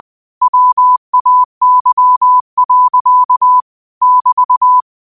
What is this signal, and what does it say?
Morse code, heard as a keyed tone, about 20 wpm, 1000 Hz. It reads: WAY. <BT>
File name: qotdfiles / 20wpm